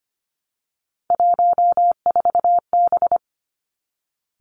Morse code, heard as a keyed tone, about 25 wpm, 700 Hz. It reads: 14B
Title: Three Challenging Letter-Number Combinations 25wpm